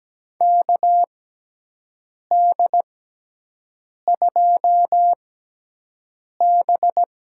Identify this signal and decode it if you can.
Morse code, heard as a keyed tone, about 17 wpm, spaced as for 6 wpm, 700 Hz. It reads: KD2B
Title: USA Call Signs - 5 Characters - 6x Character Spacing - 17wpm